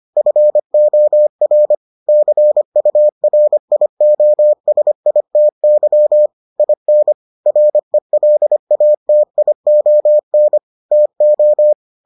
Morse code, heard as a keyed tone, about 25 wpm, 600 Hz. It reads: FOR CURIOSITY IN RELATION TO